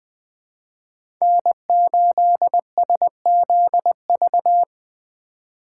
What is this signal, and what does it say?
Morse code, heard as a keyed tone, about 20 wpm, 700 Hz. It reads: N8SZV
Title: USA Call Signs - 5 Characters - 2x Character Spacing - 20wpm